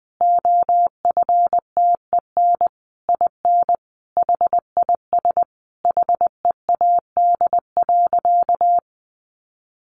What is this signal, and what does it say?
Morse code, heard as a keyed tone, about 20 wpm, 700 Hz. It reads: OFTEN IN HIS HEAD.
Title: Sentences from Top 1400 Words 20wpm